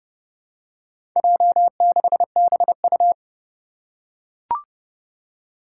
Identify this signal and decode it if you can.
Morse code, heard as a keyed tone, about 30 wpm, 700 Hz. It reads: J6BU
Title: Four Challenging Letter-Number Combinations 30wpm